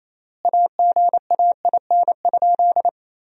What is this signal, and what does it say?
Morse code, heard as a keyed tone, about 28 wpm, 700 Hz. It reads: AGAIN?
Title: 3x Word-Spacing - Sentences from Top 200 Words - 28wpm